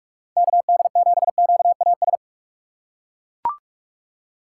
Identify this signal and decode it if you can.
Morse code, heard as a keyed tone, about 41 wpm, 700 Hz. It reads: KD7QAS
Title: USA Call Signs - 6 Characters - 2x Character Spacing - 45wpm